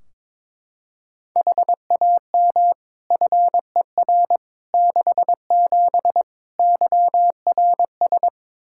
Morse code, heard as a keyed tone, about 22 wpm, 700 Hz. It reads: HAM FER 67 YRS
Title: Practice QSO - Exchange 3 - More Info - 22wpm